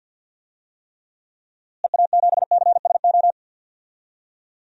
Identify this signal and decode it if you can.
Morse code, heard as a keyed tone, about 46 wpm, 700 Hz. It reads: EA7CSO
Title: DXCC-100 Call Signs - 6 Characters - Repeated 3x - 50wpm